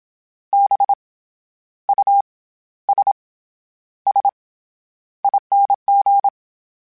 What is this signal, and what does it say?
Morse code, heard as a keyed tone, about 27 wpm, 800 Hz. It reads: B U S S ING
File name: Special INGs-mp3s / Special INGs-mp3s / SlowToFaster ing @25